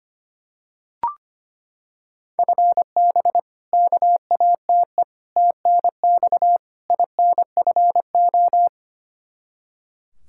Morse code, heard as a keyed tone, about 25 wpm, 700 Hz. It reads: FB KATE TNX INFO
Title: Practice QSO - Exchange 3 - More Info - 25wpm